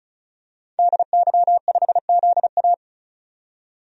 Morse code, heard as a keyed tone, about 36 wpm, 700 Hz. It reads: DY5ZA
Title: DXCC-100 Call Signs - 5 Characters - 35wpm